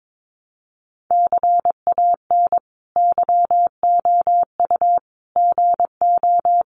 Morse code, heard as a keyed tone, about 22 wpm, 700 Hz. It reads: CAN YOU GO
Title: Sentences from Top 1500 Words 22wpm